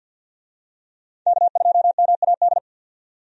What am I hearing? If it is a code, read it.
Morse code, heard as a keyed tone, about 52 wpm, 700 Hz. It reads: K2MAD
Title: USA Call Signs - 5 Characters - 50wpm